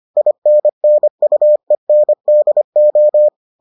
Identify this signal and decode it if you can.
Morse code, heard as a keyed tone, about 25 wpm, 600 Hz. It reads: INNUENDO